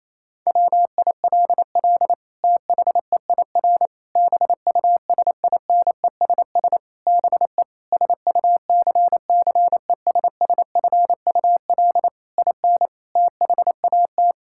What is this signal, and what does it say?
Morse code, heard as a keyed tone, about 28 wpm, 700 Hz. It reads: WILL THEIR BUSINESS BE SUCCESSFUL IN THAT
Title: Sentences from Top 1000 Words - Review Part 5 - 28wpm